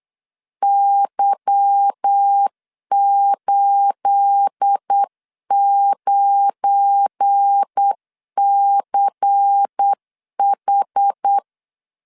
Morse code, heard as a keyed tone, about 8 wpm, 800 Hz. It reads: Y89CH
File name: LEZIONI / 9 28 8wpm 3512